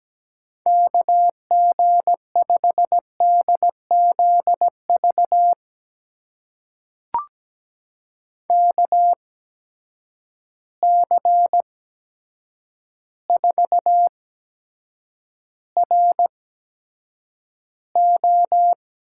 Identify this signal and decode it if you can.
Morse code, heard as a keyed tone, about 17 wpm, spaced as for 8 wpm, 700 Hz. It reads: KG5DZV KC4RO
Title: USA Call Signs - 6 Characters - 8x Character Spacing - 17wpm